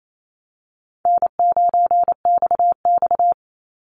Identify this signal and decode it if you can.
Morse code, heard as a keyed tone, about 28 wpm, 700 Hz. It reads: N9XX
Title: Call Signs Repeated 2x - 28wpm